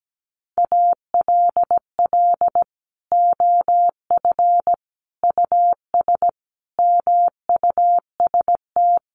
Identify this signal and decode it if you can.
Morse code, heard as a keyed tone, about 17 wpm, 700 Hz. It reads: ALL OF US MUST
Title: Sentences from Top 1400 Words 17wpm